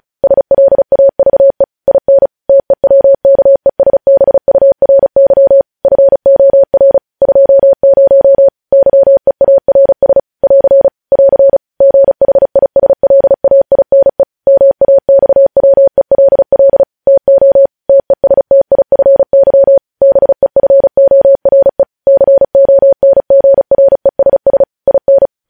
Morse code, heard as a keyed tone, about 36 wpm, 550 Hz. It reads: SLAVE IN TEWKESBURY FOR 20 YEARS <AR> <AR> GHISLAINE MAXWELL TO TESTIFY BEFORE CONGRESS IN
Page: News Headlines in Morse Code at 35 WPM